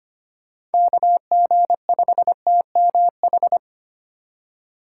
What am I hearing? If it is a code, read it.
Morse code, heard as a keyed tone, about 32 wpm, 700 Hz. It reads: KG5TMH